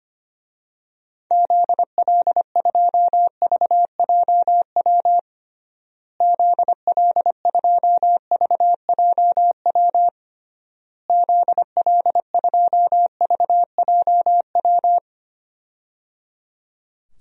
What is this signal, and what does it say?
Morse code, heard as a keyed tone, about 25 wpm, 700 Hz. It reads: ZL2VJW ZL2VJW ZL2VJW
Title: DXCC-100 Call Signs - 6 Characters - Repeated 3x - 25wpm